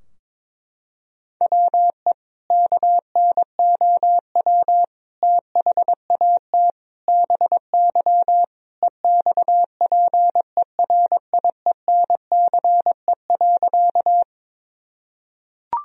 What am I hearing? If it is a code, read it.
Morse code, heard as a keyed tone, about 22 wpm, 700 Hz. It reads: WE KNOW THAT BY EXPERIENCE.
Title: Sentences from Top 1000 Words - Review Part 3 - 22wpm